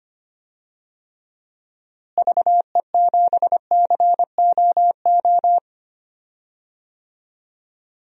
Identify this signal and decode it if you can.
Morse code, heard as a keyed tone, about 25 wpm, 700 Hz. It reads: VE7COO
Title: DXCC-100 Call Signs - 6 Characters - Repeated 3x - 25wpm